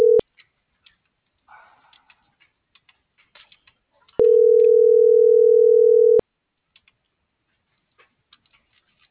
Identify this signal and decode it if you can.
Ringback tone (North America), 440 + 480 Hz.